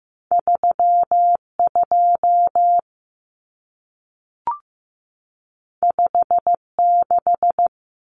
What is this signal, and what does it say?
Morse code, heard as a keyed tone, about 15 wpm, 700 Hz. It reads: 32 56
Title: Numbers - 2-Digit 15wpm